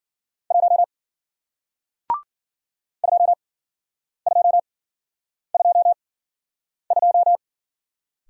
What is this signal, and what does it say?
Morse code, heard as a keyed tone, about 53 wpm, 700 Hz. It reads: 1 2 2 2 2